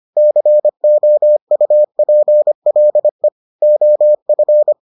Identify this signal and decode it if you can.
Morse code, heard as a keyed tone, about 25 wpm, 600 Hz. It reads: COUPLE OF